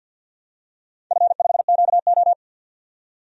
Morse code, heard as a keyed tone, about 53 wpm, 700 Hz. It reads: F5QO
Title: DXCC-100 Call Signs - 4 Characters - 50wpm